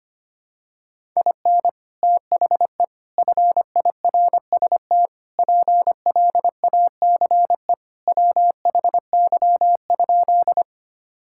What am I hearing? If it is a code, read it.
Morse code, heard as a keyed tone, about 25 wpm, 700 Hz. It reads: IN THE FIRST PLACE WHY?
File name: practice / 25wpm / Sentences with 4-Word Phrases - 25wpm